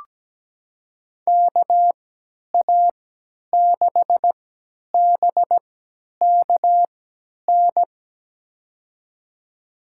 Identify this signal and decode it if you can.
Morse code, heard as a keyed tone, about 17 wpm, spaced as for 10 wpm, 700 Hz. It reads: KA6BKN
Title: USA Call Signs - 6 Characters - 3x Character Spacing - 17wpm